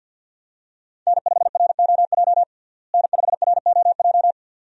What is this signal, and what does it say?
Morse code, heard as a keyed tone, about 48 wpm, 700 Hz. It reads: N5ROJ N5ROJ
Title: USA Call Signs - 5 Characters - Repeated 2x - 50wpm